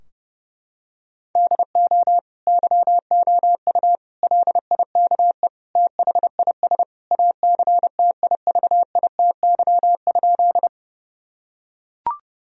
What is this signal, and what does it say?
Morse code, heard as a keyed tone, about 30 wpm, 700 Hz. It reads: DO YOU LIKE THIS ACTIVITY?
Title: Sentences from Top 1000 Words - Review Part 4 - 30wpm